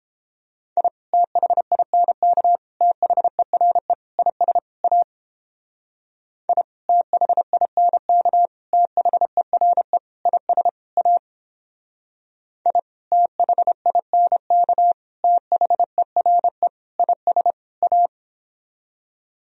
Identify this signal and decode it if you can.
Morse code, heard as a keyed tone, about 31 wpm, 700 Hz. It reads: I THINK THERE IS A I THINK THERE IS A I THINK THERE IS A